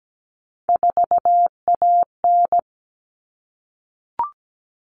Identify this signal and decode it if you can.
Morse code, heard as a keyed tone, about 17 wpm, 700 Hz. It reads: 4AN
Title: Common Cut Numbers - 3-Digit 17wpm